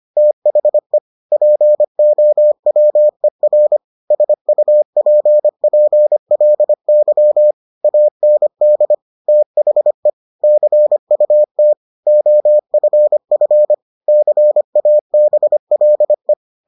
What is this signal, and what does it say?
Morse code, heard as a keyed tone, about 25 wpm, 600 Hz. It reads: THE POWER SUPPLY AND THE CUT OFF CABLE